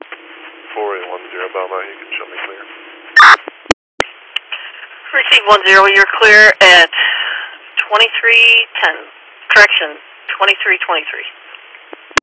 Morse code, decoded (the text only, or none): none